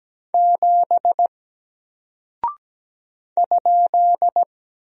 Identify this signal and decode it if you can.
Morse code, heard as a keyed tone, about 17 wpm, 700 Hz. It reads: 7 ?
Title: Single Character (Letter:Number:Punctuation:Prosign) 17wpm